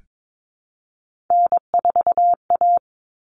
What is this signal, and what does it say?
Morse code, heard as a keyed tone, about 22 wpm, 700 Hz. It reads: N4A
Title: Common Cut Numbers - 3-Digit 22wpm